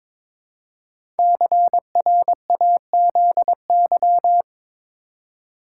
Morse code, heard as a keyed tone, about 22 wpm, 700 Hz. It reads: CRAZY
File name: 5 Letter Words - First 3 Letters Given - 22wpm